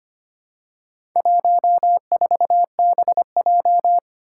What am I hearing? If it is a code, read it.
Morse code, heard as a keyed tone, about 25 wpm, 700 Hz. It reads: 14BJ
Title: Four Challenging Letter-Number Combinations 25wpm